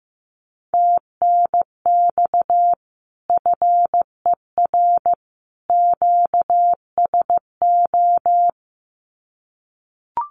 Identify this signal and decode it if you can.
Morse code, heard as a keyed tone, about 15 wpm, 700 Hz. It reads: TNX FER QSO
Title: Common QSO Phrases 15wpm